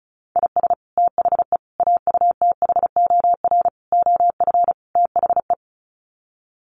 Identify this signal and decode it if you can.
Morse code, heard as a keyed tone, about 34 wpm, 700 Hz. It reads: IS THE AUTHOR OF THE
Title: Sets of 5 Words Encore 35wpm